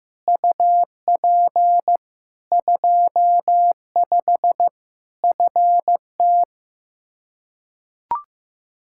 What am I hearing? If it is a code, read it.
Morse code, heard as a keyed tone, about 15 wpm, 700 Hz. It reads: UP 25 FT
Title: K1USN SST QSO Practice 15wpm